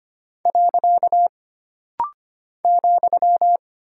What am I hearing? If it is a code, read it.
Morse code, heard as a keyed tone, about 25 wpm, 700 Hz. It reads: . ,
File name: Single Punctuation-Prosign - Rapid-Fire - 25wpm